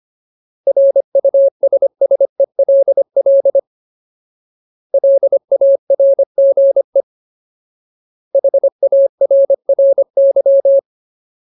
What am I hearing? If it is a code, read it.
Morse code, heard as a keyed tone, about 25 wpm, 550 Hz. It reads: RUSSELL LARGE HARRY
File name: Ultimate Word List / 25WPM Ultimate